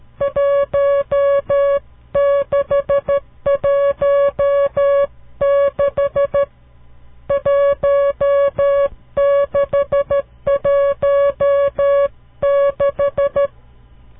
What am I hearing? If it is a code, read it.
Morse code, heard as a keyed tone, about 13 wpm, 560 Hz. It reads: 1616 1616
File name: CW Learning Programs / farnsworth / unit 5